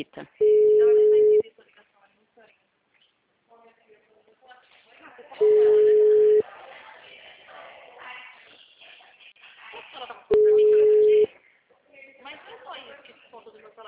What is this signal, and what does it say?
Ringback tone (Europe, 425 Hz), with speech.